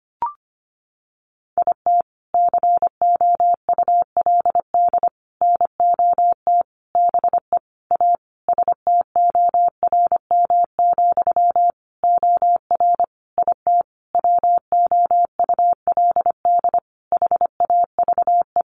Morse code, heard as a keyed tone, about 25 wpm, 700 Hz. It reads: IT COULD NOT BE A STORM, OR IT WOULD HAVE
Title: Sentences from Top 1500 Words 25wpm